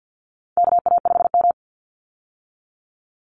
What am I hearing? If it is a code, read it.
Morse code, heard as a keyed tone, about 50 wpm, 700 Hz. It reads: KA5M
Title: Call Signs - 50wpm